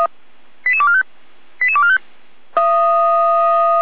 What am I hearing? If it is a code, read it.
Five-tone selective call (ZVEI), heard as quick runs of five tones, tones of about 70 ms. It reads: 88245 88245